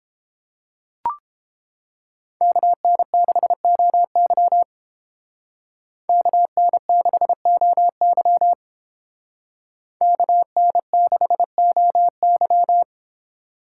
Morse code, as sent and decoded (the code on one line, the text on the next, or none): -.- -. -.... --- -.-- / -.- -. -.... --- -.-- / -.- -. -.... --- -.--
KN6OY KN6OY KN6OY